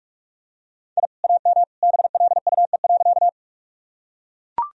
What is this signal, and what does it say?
Morse code, heard as a keyed tone, about 41 wpm, 700 Hz. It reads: I AM BLUE.
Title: Sentences from Top 1000 Words - Review Part 3 - 45wpm